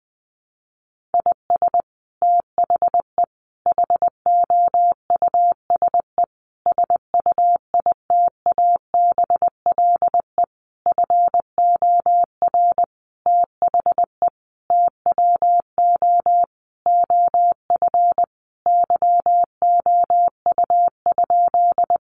Morse code, heard as a keyed tone, about 20 wpm, 700 Hz. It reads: IS THE HOUSE SUITABLE FOR THE TWO OF YOU?
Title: Sentences from Top 1400 Words 20wpm